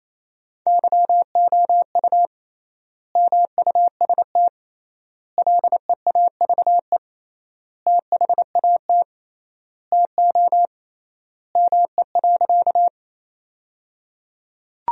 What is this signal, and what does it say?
Morse code, heard as a keyed tone, about 28 wpm, 700 Hz. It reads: YOU MUST LEAVE THAT TO ME.
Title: 3x Word-Spacing - Sentences from Top 400 Words - 28wpm